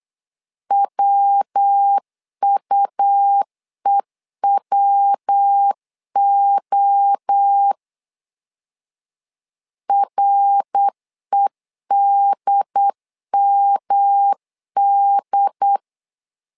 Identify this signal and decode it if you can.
Morse code, heard as a keyed tone, about 8 wpm, 800 Hz. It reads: WUEWO REDMD
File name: LEZIONI / 4 Q 8wpm 2732